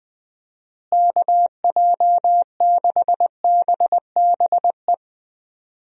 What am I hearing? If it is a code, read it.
Morse code, heard as a keyed tone, about 20 wpm, 700 Hz. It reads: KJ6BBE
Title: USA Call Signs - 6 Characters - Repeated 3x - 20wpm